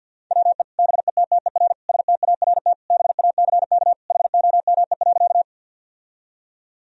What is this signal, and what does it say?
Morse code, heard as a keyed tone, about 50 wpm, 700 Hz. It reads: WE BETTER START BACK HOME.
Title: Sentences from Top 1000 Words - Review Part 1 - 50wpm